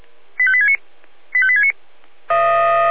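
Five-tone selective call (ZVEI), heard as quick runs of five tones, tones of about 70 ms. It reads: 85869 85869
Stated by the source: Five-tone ZVEI selective call received by SDR.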